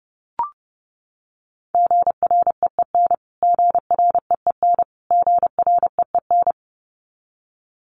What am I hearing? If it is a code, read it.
Morse code, heard as a keyed tone, about 30 wpm, 700 Hz. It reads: GREEN GREEN GREEN
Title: Top 500 Words Repeated 30wpm